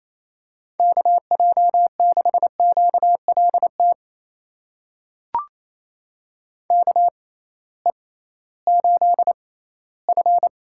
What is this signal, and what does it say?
Morse code, heard as a keyed tone, about 28 wpm, spaced as for 16 wpm, 700 Hz. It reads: KJ6QLT KE8F
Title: USA Call Signs - 6x Character Spacing - 28wpm